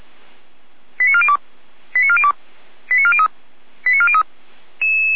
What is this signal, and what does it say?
Five-tone selective call (ZVEI), heard as quick runs of five tones, tones of about 80 ms. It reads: 89402 89402 89403 89403